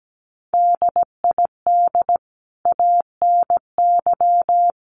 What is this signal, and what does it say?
Morse code, heard as a keyed tone, about 17 wpm, 700 Hz. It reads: DID ANY
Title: Sentences from Top 1000 Words 17wpm